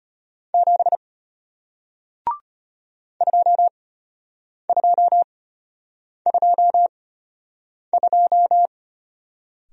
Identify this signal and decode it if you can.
Morse code, heard as a keyed tone, about 33 wpm, spaced as for 8 wpm, 700 Hz. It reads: 7 2222